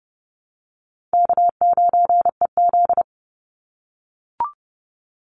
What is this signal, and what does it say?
Morse code, heard as a keyed tone, about 30 wpm, 700 Hz. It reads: K9EZ
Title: NA SOTA Chaser Callsigns 30wpm